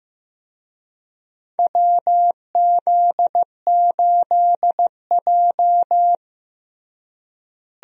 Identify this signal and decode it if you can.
Morse code, heard as a keyed tone, about 16 wpm, 700 Hz. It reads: WZ8J
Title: USA Call Signs - 4 Characters - Repeated 2x - 15wpm